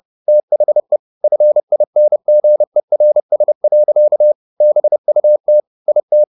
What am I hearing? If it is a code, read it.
Morse code, heard as a keyed tone, about 30 wpm, 600 Hz. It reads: THE FINGERS. BUT IT
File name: the old man and the sea / the old man and the sea 1 30wpm